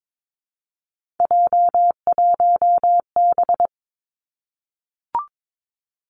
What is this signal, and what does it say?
Morse code, heard as a keyed tone, about 22 wpm, 700 Hz. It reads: J1B
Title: Three Challenging Letter-Number Combinations 22wpm